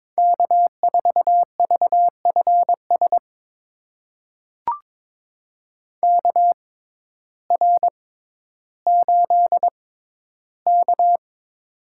Morse code, heard as a keyed tone, about 22 wpm, spaced as for 12 wpm, 700 Hz. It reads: K4VFS KR8K
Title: USA Call Signs - 6x Character Spacing - 22wpm